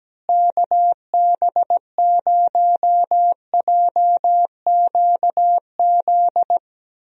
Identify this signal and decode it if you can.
Morse code, heard as a keyed tone, about 17 wpm, 700 Hz. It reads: KB0JQZ
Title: USA Call Signs - 6 Characters - Repeated 3x - 17wpm